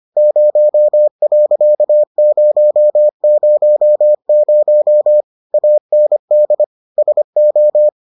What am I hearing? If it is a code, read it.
Morse code, heard as a keyed tone, about 25 wpm, 600 Hz. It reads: 0.000 AND SO